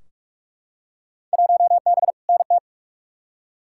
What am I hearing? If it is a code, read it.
Morse code, heard as a keyed tone, about 42 wpm, 700 Hz. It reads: 1B NT
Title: Field Day Exchanges Encore - 45wpm